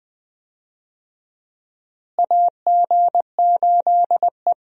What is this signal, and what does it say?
Morse code, heard as a keyed tone, about 21 wpm, 700 Hz. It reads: AG8E
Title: USA Call Signs - 4 Characters - Repeated 2x - 20wpm